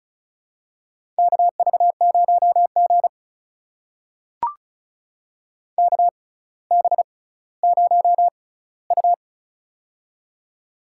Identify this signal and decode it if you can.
Morse code, heard as a keyed tone, about 36 wpm, spaced as for 18 wpm, 700 Hz. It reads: KV0G KB0U
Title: USA Call Signs - 4 Characters - 6x Character Spacing - 35wpm